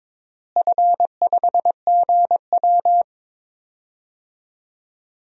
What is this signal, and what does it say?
Morse code, heard as a keyed tone, about 22 wpm, 700 Hz. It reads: F5GW
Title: DXCC-100 Call Signs - 4 Characters - Repeated 2x - 22wpm